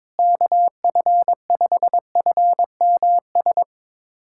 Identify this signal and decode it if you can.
Morse code, heard as a keyed tone, about 21 wpm, 700 Hz. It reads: KF5FMS
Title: USA Callsign DE Callsign 22wpm